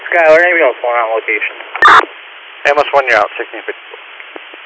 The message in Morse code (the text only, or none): none